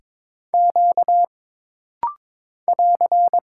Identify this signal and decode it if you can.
Morse code, heard as a keyed tone, about 22 wpm, 700 Hz. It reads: Q <AR>
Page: This practice set is composed of Letters, Numbers, common punctuations, and common prosigns.